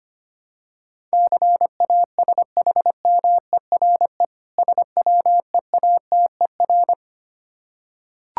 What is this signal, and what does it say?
Morse code, heard as a keyed tone, about 25 wpm, 700 Hz. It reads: CASHMERE SWEATER
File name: practice / 25wpm / Difficult Words with Hints - 25wpm